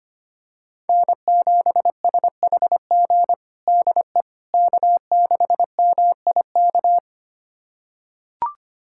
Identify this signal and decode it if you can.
Morse code, heard as a keyed tone, about 25 wpm, 700 Hz. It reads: N7SHG DE K6MIK
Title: USA Callsign DE Callsign - 5 Characters - 25wpm